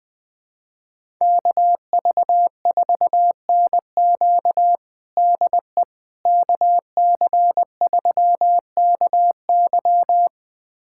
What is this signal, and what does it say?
Morse code, heard as a keyed tone, about 20 wpm, 700 Hz. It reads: KV4NQ DE KC3KY
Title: USA Callsign DE Callsign 20wpm